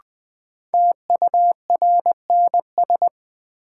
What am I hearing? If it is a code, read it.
Morse code, heard as a keyed tone, about 20 wpm, 700 Hz. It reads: TURNS
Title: Sets of 5 Words Encore - New Word Drill - 20wpm